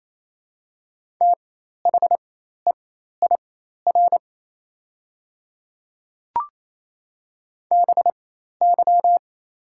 Morse code, heard as a keyed tone, about 28 wpm, spaced as for 13 wpm, 700 Hz. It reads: THEIR BY
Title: 4x Character Spacing - Top 100 Words - 28wpm